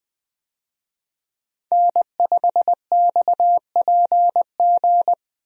Morse code, heard as a keyed tone, about 21 wpm, 700 Hz. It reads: N5XPG
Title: USA Call Signs - 5 Characters - Repeated 3x - 20wpm